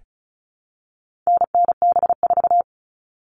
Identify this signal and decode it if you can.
Morse code, heard as a keyed tone, about 34 wpm, 700 Hz. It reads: NNB4
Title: All Cut Numbers - 4-Digit 35wpm